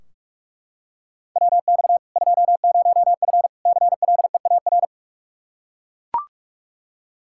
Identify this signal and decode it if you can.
Morse code, heard as a keyed tone, about 41 wpm, 700 Hz. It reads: WX 20F CLEAR
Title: K1USN SST QSO Practice 45wpm